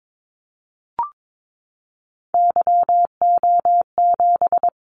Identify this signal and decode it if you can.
Morse code, heard as a keyed tone, about 22 wpm, 700 Hz. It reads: YO7
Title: Callsign Prefixes for Top 25 DXCC - 22wpm